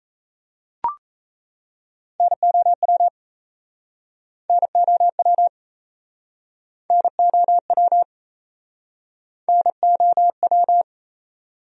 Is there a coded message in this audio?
Yes: Morse code from 2.2 to 10.8 s, heard as a keyed tone, about 39 wpm, 700 Hz. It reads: NOW NOW NOW NOW